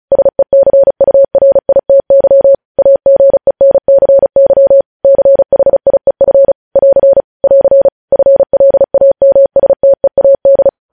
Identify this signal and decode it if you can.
Morse code, heard as a keyed tone, about 36 wpm, 550 Hz. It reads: SECURITY AGENCY CHIEF <AR> <AR> FLAMSTEAD
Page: news headlines in morse code at 35 wpm